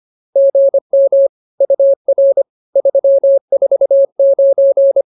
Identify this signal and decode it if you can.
Morse code, heard as a keyed tone, about 24 wpm, 550 Hz. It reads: GM UR 349